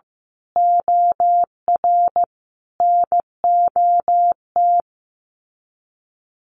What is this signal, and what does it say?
Morse code, heard as a keyed tone, about 15 wpm, 700 Hz. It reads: OR NOT
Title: Sets of 5 Words Encore 15wpm